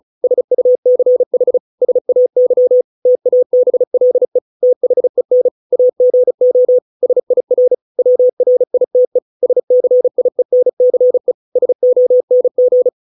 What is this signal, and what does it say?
Morse code, heard as a keyed tone, about 34 wpm, 500 Hz. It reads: SUCH SAY TABLE THEN AGO SIR WRITE SCIENCE SONG